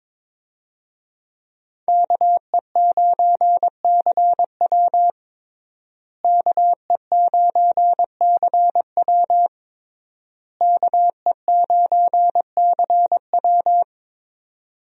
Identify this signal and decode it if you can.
Morse code, heard as a keyed tone, about 22 wpm, 700 Hz. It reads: KE9CW KE9CW KE9CW
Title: USA Call Signs - 5 Characters - Repeated 3x - 22wpm